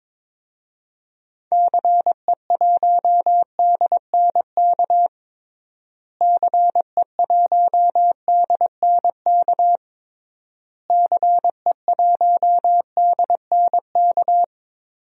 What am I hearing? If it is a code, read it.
Morse code, heard as a keyed tone, about 22 wpm, 700 Hz. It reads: CE1DNK CE1DNK CE1DNK
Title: DXCC-100 Call Signs - 6 Characters - Repeated 3x - 22wpm